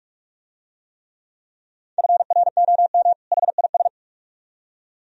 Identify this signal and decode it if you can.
Morse code, heard as a keyed tone, about 44 wpm, 700 Hz. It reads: FROM HIS